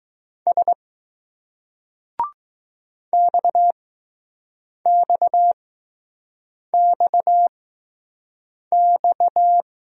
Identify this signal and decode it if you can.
Morse code, heard as a keyed tone, about 20 wpm, spaced as for 6 wpm, 700 Hz. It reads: S XXXX